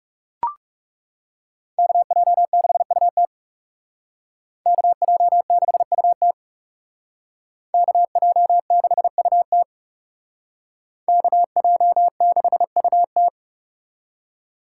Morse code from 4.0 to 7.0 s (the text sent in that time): KJ6UT